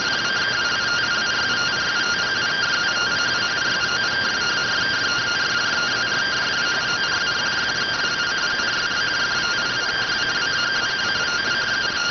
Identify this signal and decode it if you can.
Two-tone teleprinter signal, about 170 Hz shift.